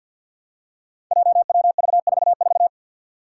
Morse code, heard as a keyed tone, about 52 wpm, 700 Hz. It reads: JWV44
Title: Five Challenging Letter-Number Combinations 50wpm